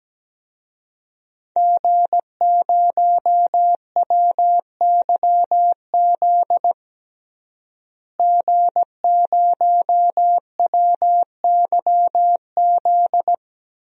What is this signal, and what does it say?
Morse code, heard as a keyed tone, about 17 wpm, 700 Hz. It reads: G0WYZ G0WYZ
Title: DXCC-100 Call Signs - 5 Characters - Repeated 3x - 17wpm